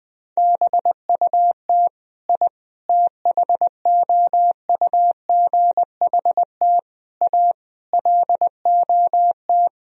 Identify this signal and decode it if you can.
Morse code, heard as a keyed tone, about 20 wpm, 700 Hz. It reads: BUT I THOUGHT A LOT
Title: Sentences from Top 1000 Words - Review Part 3 - 20wpm